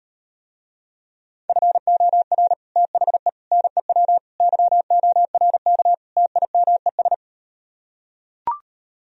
Morse code, heard as a keyed tone, about 38 wpm, 700 Hz. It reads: FOR THE NEW YORK TIMES